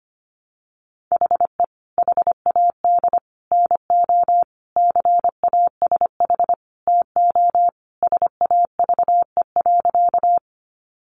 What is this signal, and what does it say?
Morse code, heard as a keyed tone, about 25 wpm, 690 Hz. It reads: HE HAD NO CASH TO SAVE.
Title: Sentences from Top 1100 Words 25wpm